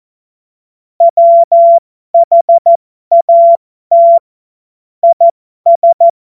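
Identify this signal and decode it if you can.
Morse code, heard as a keyed tone, about 14 wpm, spaced as for 12 wpm, 680 Hz. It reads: WHAT IS
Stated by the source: Morse joke practice at effective 60, Farnsworth 70.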